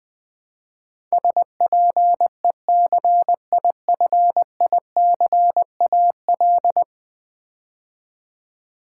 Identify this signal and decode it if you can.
Morse code, heard as a keyed tone, about 20 wpm, 700 Hz. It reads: SPECIFICAL
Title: Build A Word - 5 Letters At A Time - 12-Letter Words - 20wpm